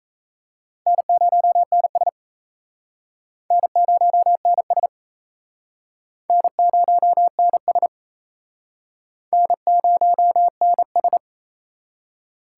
Morse code, heard as a keyed tone, about 35 wpm, 700 Hz. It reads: N0NS N0NS N0NS N0NS